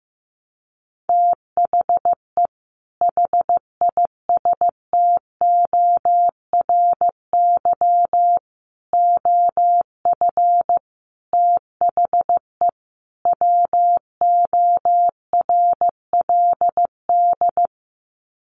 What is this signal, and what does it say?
Morse code, heard as a keyed tone, about 15 wpm, 700 Hz. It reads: THE HISTORY OF THE WORLD
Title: Sets of 5 Words Encore2 15wpm